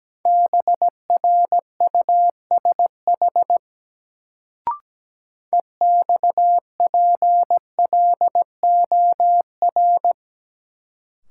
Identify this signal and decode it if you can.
Morse code, heard as a keyed tone, about 17 wpm, 700 Hz. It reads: BRUSH EXPLOR
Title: Sentences from Top 1100 Words - New Word Drill - 17wpm